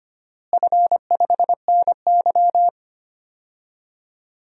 Morse code, heard as a keyed tone, about 25 wpm, 700 Hz. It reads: F5NY
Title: DXCC-100 Call Signs - 4 Characters - Repeated 2x - 25wpm